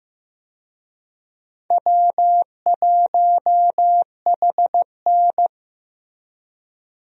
Morse code, heard as a keyed tone, about 15 wpm, 700 Hz. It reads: W1HN
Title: USA Call Signs - 4 Characters - Repeated 3x - 15wpm